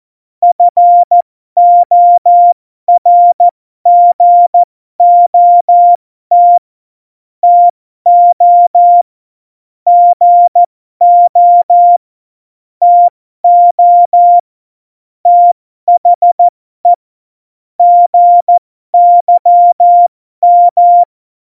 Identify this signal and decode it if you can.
Morse code, heard as a keyed tone, about 14 wpm, spaced as for 12 wpm, 700 Hz. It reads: FORGOT TO GO TO THE GYM
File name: Morsekurs / Witze